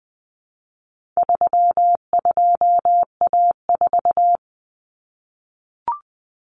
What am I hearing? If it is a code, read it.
Morse code, heard as a keyed tone, about 20 wpm, 700 Hz. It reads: 32A4
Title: Common Cut Numbers - 4-Digit 20wpm